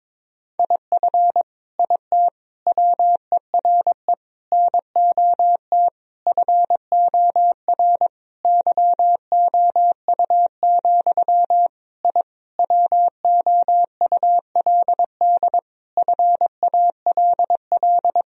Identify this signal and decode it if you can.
Morse code, heard as a keyed tone, about 21 wpm, 700 Hz. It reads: IF IT WERE NOT FOR YOU, I WOULD FALL
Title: Sentences from Top 1000 Words - Review Part 2 - 22wpm